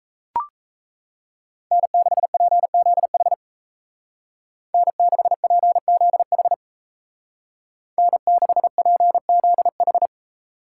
Morse code, heard as a keyed tone, about 33 wpm, 700 Hz. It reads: N6PZH N6PZH N6PZH
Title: Speed Racing - USA 1x3 Call Signs 28wpm